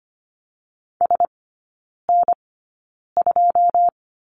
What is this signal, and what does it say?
Morse code, heard as a keyed tone, about 25 wpm, spaced as for 9 wpm, 690 Hz. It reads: SN2
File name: Lesson 14e - Number 2 - Three Characters 25wpm